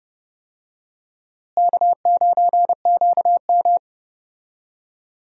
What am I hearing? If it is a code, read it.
Morse code, heard as a keyed tone, about 32 wpm, 700 Hz. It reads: K9QM
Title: USA Call Signs - 4 Characters - Repeated 2x - 30wpm